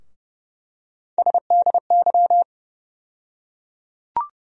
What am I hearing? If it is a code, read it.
Morse code, heard as a keyed tone, about 32 wpm, 700 Hz. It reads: SDY
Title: USA Call Sign Postfixes - 3 Letters 30wpm